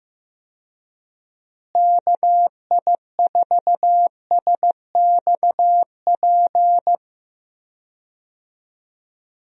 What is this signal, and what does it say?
Morse code, heard as a keyed tone, about 15 wpm, 700 Hz. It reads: KI4SXP
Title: USA Call Signs - 6 Characters - Repeated 3x - 15wpm